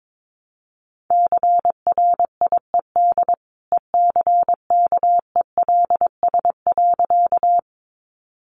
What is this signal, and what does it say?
Morse code, heard as a keyed tone, about 22 wpm, 700 Hz. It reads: CRIED ECKELS.
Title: A Sound Of Thunder 22wpm